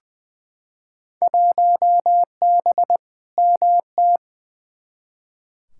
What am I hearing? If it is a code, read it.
Morse code, heard as a keyed tone, about 20 wpm, 700 Hz. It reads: 1B MT
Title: Field Day Exchanges - 20wpm